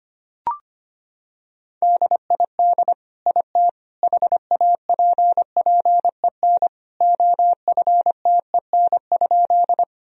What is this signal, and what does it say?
Morse code, heard as a keyed tone, about 25 wpm, 700 Hz. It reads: DID IT HAPPEN OFTEN?
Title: Sentences from Top 1000 Words - Review Part 1 - 25wpm